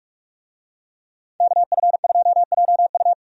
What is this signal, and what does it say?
Morse code, heard as a keyed tone, about 46 wpm, 700 Hz. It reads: KF2JU
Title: USA Call Signs - 5 Characters - 45wpm